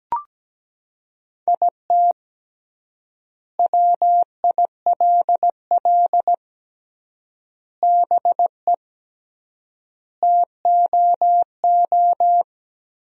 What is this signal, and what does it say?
Morse code, heard as a keyed tone, about 17 wpm, 700 Hz. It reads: IT WILL BE TOO
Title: 3x Word-Spacing - Sentences from Top 300 Words - 17wpm